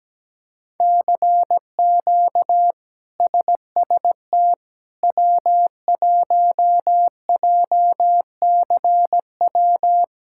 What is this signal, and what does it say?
Morse code, heard as a keyed tone, about 17 wpm, 700 Hz. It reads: CQ SST W1JCW
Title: K1USN SST QSO Practice - 17wpm